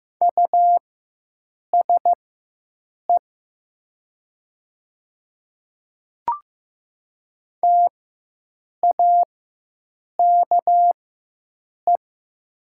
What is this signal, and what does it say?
Morse code, heard as a keyed tone, about 15 wpm, spaced as for 7 wpm, 700 Hz. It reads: USE TAKE
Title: 4x Character Spacing - Top 200 Words - 15wpm